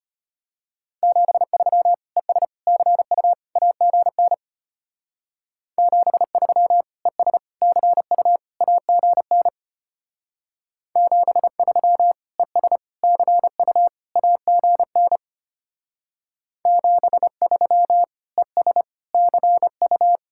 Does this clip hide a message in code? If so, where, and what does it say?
Yes: Morse code from 1.0 to 20.2 s, heard as a keyed tone, about 33 wpm, 700 Hz. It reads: 73 ES CU AGN 73 ES CU AGN 73 ES CU AGN 73 ES CU